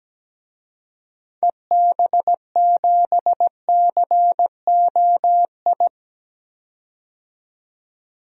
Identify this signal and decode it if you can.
Morse code, heard as a keyed tone, about 17 wpm, 700 Hz. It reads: EB7COI
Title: DXCC-100 Call Signs - 6 Characters - 17wpm